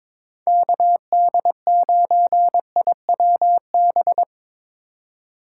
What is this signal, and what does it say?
Morse code, heard as a keyed tone, about 25 wpm, 700 Hz. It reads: KD9IWB
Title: Speed Racing - USA Call Signs 22wpm